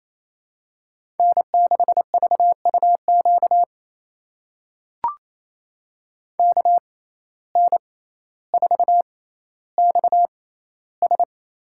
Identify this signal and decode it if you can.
Morse code, heard as a keyed tone, about 28 wpm, spaced as for 14 wpm, 700 Hz. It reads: N6VUQ KN4XS
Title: USA Call Signs - 5 Characters - 6x Character Spacing - 28wpm